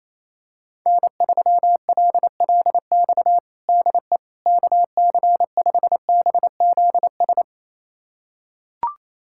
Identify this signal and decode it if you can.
Morse code, heard as a keyed tone, about 27 wpm, 700 Hz. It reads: N3LLX DE KC5BZS
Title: USA Callsign DE Callsign 28wpm